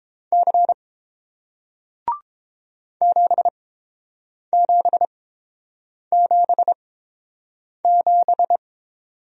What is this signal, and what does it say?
Morse code, heard as a keyed tone, about 31 wpm, spaced as for 7 wpm, 700 Hz. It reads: C 7777